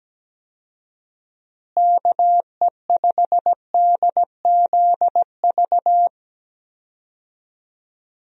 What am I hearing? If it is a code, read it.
Morse code, heard as a keyed tone, about 17 wpm, 700 Hz. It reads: KE5DZV
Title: USA Call Signs - 6 Characters - Repeated 2x - 17wpm